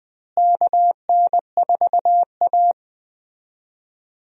Morse code, heard as a keyed tone, about 20 wpm, 700 Hz. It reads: KN4A
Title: USA Call Signs - 4 Characters - Repeated 3x - 20wpm